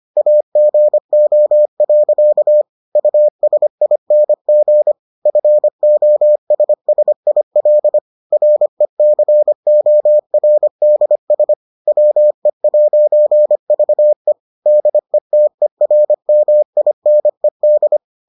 Morse code, heard as a keyed tone, about 25 wpm, 600 Hz. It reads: AGO. USING FOSSIL RECORDS WE'VE DETERMINED